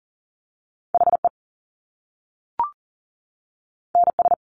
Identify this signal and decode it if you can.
Morse code, heard as a keyed tone, about 39 wpm, 700 Hz. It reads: HE NS
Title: Top Two Letter Combinations in Words 40wpm